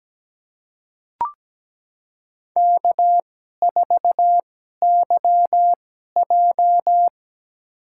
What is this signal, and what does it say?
Morse code, heard as a keyed tone, about 17 wpm, spaced as for 12 wpm, 700 Hz. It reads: K4YJ
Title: USA Call Signs - 4 Characters - 2x Character Spacing - 17wpm